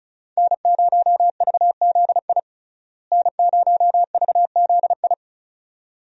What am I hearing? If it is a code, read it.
Morse code, heard as a keyed tone, about 36 wpm, 700 Hz. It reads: N0VZI N0VZI
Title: USA Call Signs - 5 Characters - Repeated 2x - 35wpm